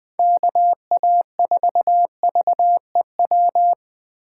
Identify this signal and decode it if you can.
Morse code, heard as a keyed tone, about 20 wpm, 700 Hz. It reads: KA4VEW
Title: USA Call Signs - 6 Characters - 4x Character Spacing - 20wpm